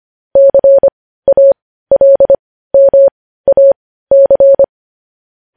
Morse code, heard as a keyed tone, about 25 wpm, spaced as for 15 wpm, 550 Hz. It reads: CALMAC
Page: news headlines in morse code at 15 wpm